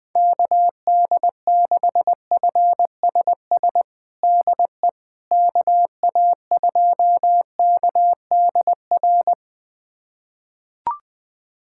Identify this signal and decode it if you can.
Morse code, heard as a keyed tone, about 20 wpm, 700 Hz. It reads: KD6FSS DE KA2KDR
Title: USA Callsign DE Callsign 20wpm